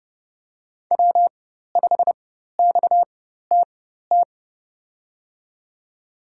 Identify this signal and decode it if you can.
Morse code, heard as a keyed tone, about 32 wpm, spaced as for 14 wpm, 700 Hz. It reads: W5XTT
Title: USA Call Signs - 5 Characters - 4x Character Spacing - 30wpm